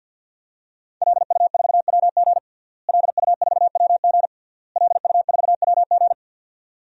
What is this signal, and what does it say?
Morse code, heard as a keyed tone, about 49 wpm, 700 Hz. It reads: LU4WG LU4WG LU4WG
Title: DXCC-100 Call Signs - 5 Characters - Repeated 3x - 50wpm